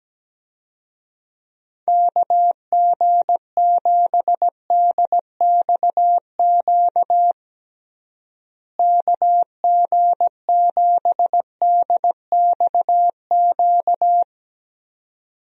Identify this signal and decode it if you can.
Morse code, heard as a keyed tone, about 17 wpm, 700 Hz. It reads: KG7DXQ KG7DXQ
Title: USA Call Signs - 6 Characters - Repeated 2x - 17wpm